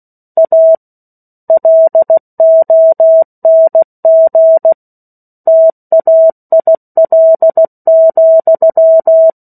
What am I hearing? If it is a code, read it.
Morse code, heard as a keyed tone, about 16 wpm, 650 Hz. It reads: A LONG TAIL,
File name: 03-Chapter-03 16 WPM